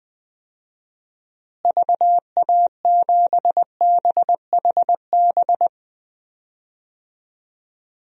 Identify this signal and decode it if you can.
Morse code, heard as a keyed tone, about 20 wpm, 700 Hz. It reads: VA7BHB
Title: DXCC-100 Call Signs - 6 Characters - Repeated 3x - 20wpm